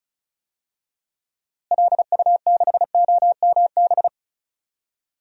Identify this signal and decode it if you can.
Morse code, heard as a keyed tone, about 36 wpm, 700 Hz. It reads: LU6OMB
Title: DXCC-100 Call Signs - 6 Characters - Repeated 3x - 35wpm